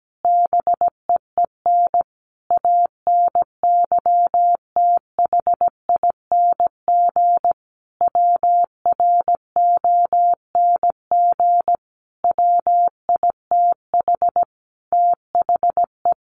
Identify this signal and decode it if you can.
Morse code, heard as a keyed tone, about 17 wpm, 700 Hz. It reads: BEEN ANYTHING WRONG WITH THE
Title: Sentences from Top 1500 Words 17wpm